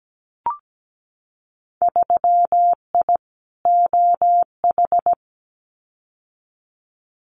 Morse code, heard as a keyed tone, about 17 wpm, 700 Hz. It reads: 3I OH
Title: Winter Field Day Exchanges Encore 17wpm